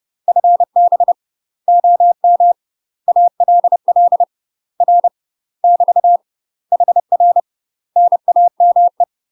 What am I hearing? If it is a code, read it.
Morse code, heard as a keyed tone, about 30 wpm, 700 Hz. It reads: FB OM ALL R <BT> HR NAME
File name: UK MASTER QSOs TRAINING / 30WPM / UK MASTER QSOs 0027